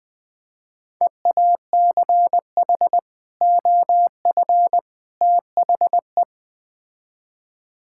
Morse code, heard as a keyed tone, about 20 wpm, 700 Hz. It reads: EACH OF THE
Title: Build A Phrase - 3-Word Phrases - 20wpm